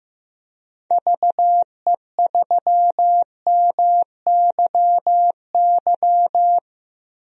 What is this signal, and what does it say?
Morse code, heard as a keyed tone, about 15 wpm, 700 Hz. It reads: VE3MYY
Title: DXCC-100 Call Signs - 6 Characters - 15wpm